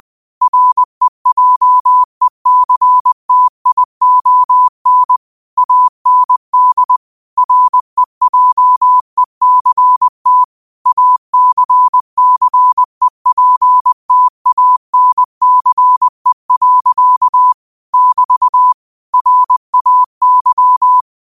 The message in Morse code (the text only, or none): REJECTION AND REJECT ACCEPTANCE. <BT> RAY